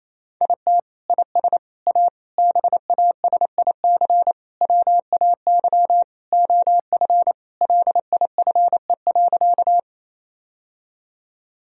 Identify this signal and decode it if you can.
Morse code, heard as a keyed tone, about 28 wpm, 700 Hz. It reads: IT IS A BASIC WAY OF LIFE.
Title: Sentences from Top 1000 Words - Review Part 3 - 28wpm